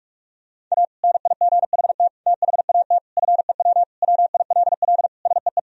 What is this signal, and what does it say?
Morse code, heard as a keyed tone, about 44 wpm, 700 Hz. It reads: A NIGHT THAT FEW WILL SEE
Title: Sentences from Top 1000 Words - Review Part 1 - 45wpm